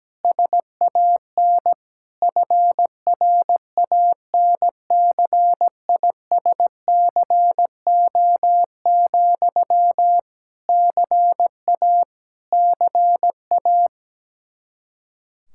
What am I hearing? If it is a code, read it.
Morse code, heard as a keyed tone, about 17 wpm, 700 Hz. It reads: SAN FRANCISCO, CA CA
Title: Practice QSO - Exchange 1 - Beginning - 17wpm